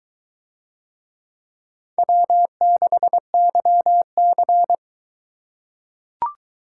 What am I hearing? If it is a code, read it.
Morse code, heard as a keyed tone, about 21 wpm, 700 Hz. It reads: W6YC